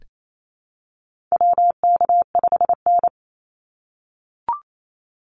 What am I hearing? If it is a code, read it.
Morse code, heard as a keyed tone, about 28 wpm, 700 Hz. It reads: WK5N
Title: Call Signs - 28wpm